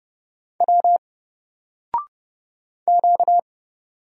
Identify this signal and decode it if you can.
Morse code, heard as a keyed tone, about 29 wpm, 700 Hz. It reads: W Q
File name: Single-Letter-Number-30wpm